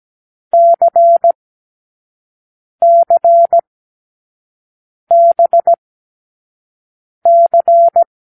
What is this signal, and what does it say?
Morse code, heard as a keyed tone, about 17 wpm, 680 Hz. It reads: C C B C
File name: k7qo / Lesson6(A-D)